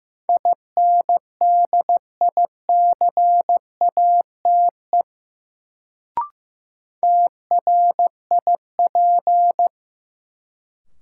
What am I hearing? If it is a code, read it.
Morse code, heard as a keyed tone, about 15 wpm, 700 Hz. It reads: INDICATE TRIP
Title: Sentences from Top 1300 Words - New Word Drill Alt - 15wpm